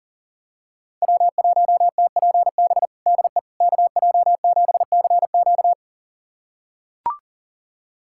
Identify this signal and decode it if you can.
Morse code, heard as a keyed tone, about 37 wpm, 700 Hz. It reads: W1TPB DE KJ7CQ
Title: USA Callsign DE Callsign - 5 Characters - 40wpm